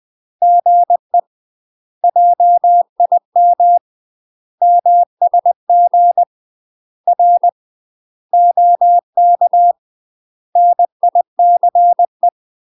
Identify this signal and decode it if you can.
Morse code, heard as a keyed tone, about 20 wpm, 700 Hz. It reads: GE JIM MSG R OK NICE
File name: UK MASTER QSOs TRAINING / 20WPM / UK MASTER QSOs 0039